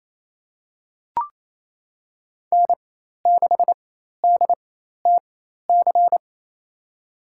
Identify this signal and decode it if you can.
Morse code, heard as a keyed tone, about 28 wpm, spaced as for 13 wpm, 700 Hz. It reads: N6DTC
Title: USA Call Signs - 5 Characters - 4x Character Spacing - 28wpm